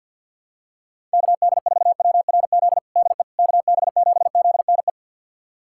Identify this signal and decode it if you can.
Morse code, heard as a keyed tone, about 53 wpm, 700 Hz. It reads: KD4WRZ DE KB7ZNE
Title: USA Callsign DE Callsign - 6 Characters - 50wpm